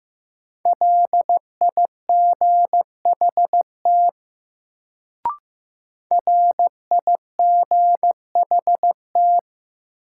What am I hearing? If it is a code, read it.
Morse code, heard as a keyed tone, about 15 wpm, 700 Hz. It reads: LIGHT RIGHT
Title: Sentences from Top 200 Words - New Word Drill - 15wpm